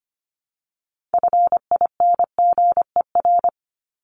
Morse code, heard as a keyed tone, about 25 wpm, 700 Hz. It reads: FINGER
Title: Top 1501-1750 Words 25wpm